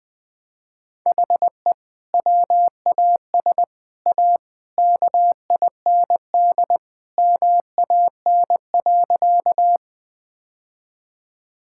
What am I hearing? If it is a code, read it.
Morse code, heard as a keyed tone, about 20 wpm, 700 Hz. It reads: HE WAS A KIND MAN.
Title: Sentences with 3-Word Phrases - 20wpm